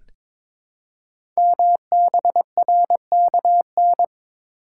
Morse code, heard as a keyed tone, about 22 wpm, 700 Hz. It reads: MBRKN
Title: Code Groups - 5 Letters - 22wpm